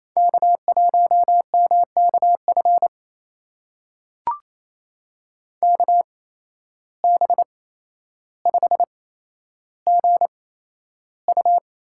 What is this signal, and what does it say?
Morse code, heard as a keyed tone, about 28 wpm, spaced as for 12 wpm, 700 Hz. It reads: K1MKF KB5GU
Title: USA Call Signs - 5 Characters - 8x Character Spacing - 28wpm